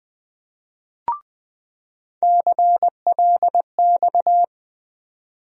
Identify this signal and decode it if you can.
Morse code, heard as a keyed tone, about 20 wpm, 700 Hz. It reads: CLX
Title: USA Call Sign Postfixes - 3 Letters 20wpm